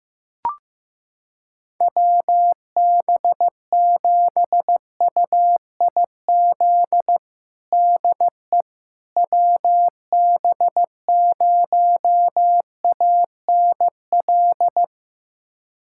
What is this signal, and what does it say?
Morse code, heard as a keyed tone, about 14 wpm, 700 Hz. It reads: WB7UIZ DE WB0ANL
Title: USA Callsign DE Callsign - 6 Characters - 15wpm